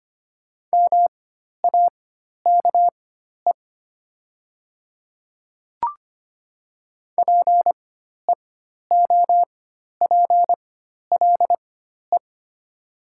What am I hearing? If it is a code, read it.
Morse code, heard as a keyed tone, about 25 wpm, spaced as for 12 wpm, 700 Hz. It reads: MAKE PEOPLE
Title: 4x Character Spacing - Top 200 Words - 25wpm